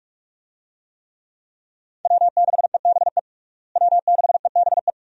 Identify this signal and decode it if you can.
Morse code, heard as a keyed tone, about 43 wpm, 700 Hz. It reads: W6EBE W6EBE
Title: USA Call Signs - 5 Characters - Repeated 2x - 45wpm